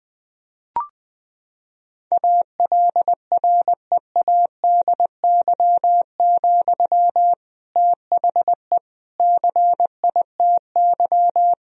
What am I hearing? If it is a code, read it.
Morse code, heard as a keyed tone, about 21 wpm, 700 Hz. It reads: ALREADY, THE CITY
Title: Sentences from Top 1000 Words - Review Part 4 - 20wpm